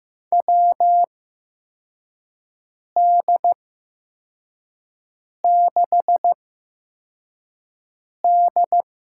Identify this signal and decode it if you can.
Morse code, heard as a keyed tone, about 15 wpm, spaced as for 4 wpm, 700 Hz. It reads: WD6D
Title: USA Call Signs - 8x Character Spacing - 15wpm